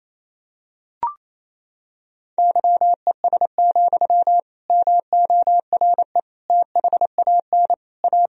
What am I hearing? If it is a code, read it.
Morse code, heard as a keyed tone, about 28 wpm, 700 Hz. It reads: YES, MORE THAN A
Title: Sentences from Top 1000 Words - Review Part 1 - 28wpm